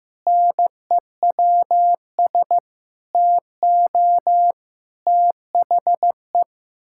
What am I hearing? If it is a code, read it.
Morse code, heard as a keyed tone, about 15 wpm, 700 Hz. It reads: NEWS TO THE
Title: Pangram Challenge v2 - 15wpm